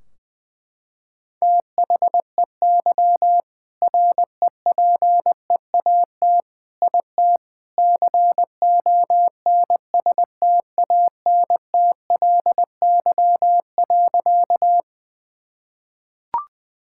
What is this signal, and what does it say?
Morse code, heard as a keyed tone, about 21 wpm, 700 Hz. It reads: THEY REPEAT IT CONSTANTLY.
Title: Sentences from Top 1600 Words 20wpm